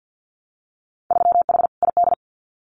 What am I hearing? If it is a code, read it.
Morse code, heard as a keyed tone, about 50 wpm, 700 Hz. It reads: 3H ID
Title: Winter Field Day Exchanges Encore 50wpm